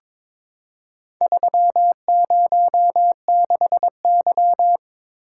Morse code, heard as a keyed tone, about 22 wpm, 700 Hz. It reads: 306Y
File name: practice / 22wpm / Four Letter-Number Combinations - 22wpm